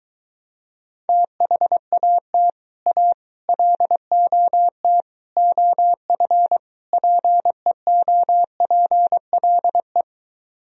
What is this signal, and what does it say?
Morse code, heard as a keyed tone, about 22 wpm, 700 Hz. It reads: THAT A LOT OF PEOPLE